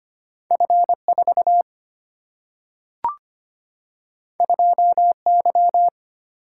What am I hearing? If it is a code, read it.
Morse code, heard as a keyed tone, about 25 wpm, 700 Hz. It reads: F 4 2 Y
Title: Single-Letter Single-Number 25wpm